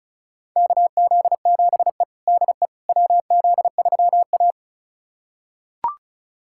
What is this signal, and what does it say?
Morse code, heard as a keyed tone, about 34 wpm, 700 Hz. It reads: KZ7E DE WZ3A
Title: USA Callsign DE Callsign - 4 Characters - 35wpm